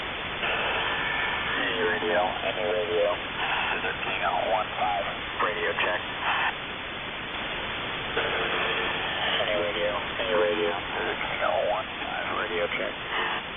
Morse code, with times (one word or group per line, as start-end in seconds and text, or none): none